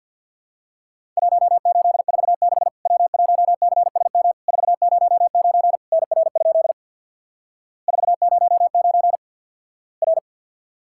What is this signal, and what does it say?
Morse code, heard as a keyed tone, about 50 wpm, 700 Hz. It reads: 1846 W1CSM 409 NR? 409 R
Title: Worked All Europe DX Contest - QTC Exchange with Repeats - 50wpm